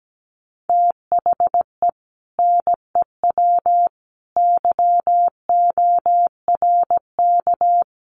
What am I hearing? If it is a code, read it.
Morse code, heard as a keyed tone, about 17 wpm, 700 Hz. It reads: THE NEW YORK
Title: Sets of 5 Words 17wpm